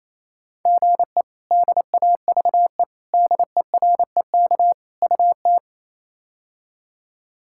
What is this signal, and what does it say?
Morse code, heard as a keyed tone, about 28 wpm, 700 Hz. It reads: GE DAVE DEREK UT
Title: K1USN SST QSO Practice - 28wpm